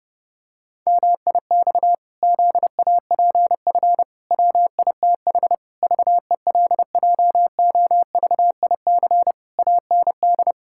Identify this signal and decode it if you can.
Morse code, heard as a keyed tone, about 30 wpm, 700 Hz. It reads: MIX ZAPF WITH VELJOVIC AND
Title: Pangram Challenge v2 - 30wpm